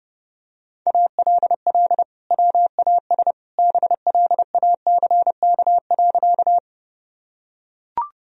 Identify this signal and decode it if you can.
Morse code, heard as a keyed tone, about 30 wpm, 700 Hz. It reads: ALL WAS BLACK.
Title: Sentences from Top 1000 Words - Review Part 2 - 30wpm